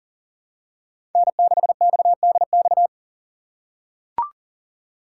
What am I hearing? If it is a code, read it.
Morse code, heard as a keyed tone, about 40 wpm, 700 Hz. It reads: N6XDX
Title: USA Call Signs - 5 Characters - 6x Character Spacing - 40wpm